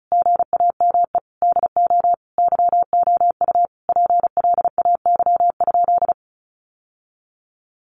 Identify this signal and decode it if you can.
Morse code, heard as a keyed tone, about 34 wpm, 700 Hz. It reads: GAME DO YOU PLAY?
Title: Sentences from Top 1400 Words 35wpm